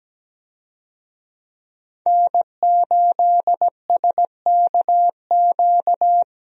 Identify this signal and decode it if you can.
Morse code, heard as a keyed tone, about 17 wpm, 700 Hz. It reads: N8SKQ
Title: USA Call Signs - 5 Characters - 17wpm